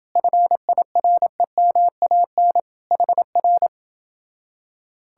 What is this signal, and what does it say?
Morse code, heard as a keyed tone, about 29 wpm, 700 Hz. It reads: FIREMAN HR